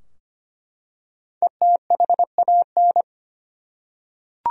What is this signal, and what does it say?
Morse code, heard as a keyed tone, about 25 wpm, 700 Hz. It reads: ETHAN
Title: USA Common Names - 25wpm